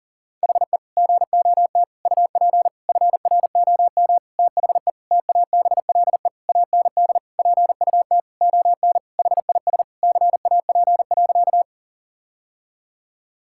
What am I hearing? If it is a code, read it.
Morse code, heard as a keyed tone, about 37 wpm, 700 Hz. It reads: HE GOT UP FROM THE TABLE AND PUT ON HIS CAP.
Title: Sentences from Top 1600 Words 40wpm